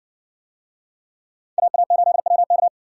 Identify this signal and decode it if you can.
Morse code, heard as a keyed tone, about 60 wpm, 700 Hz. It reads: AA8WK